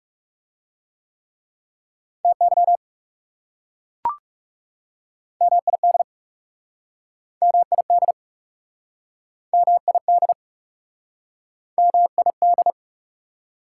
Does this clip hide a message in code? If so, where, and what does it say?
Yes: Morse code from 2.2 to 12.7 s, heard as a keyed tone, about 40 wpm, 700 Hz. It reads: TY MID MID MID MID